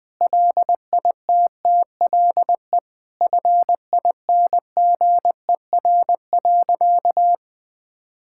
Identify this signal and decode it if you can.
Morse code, heard as a keyed tone, about 20 wpm, 700 Hz. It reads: LITTLE FINGER.
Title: Sentences from Top 1600 Words 20wpm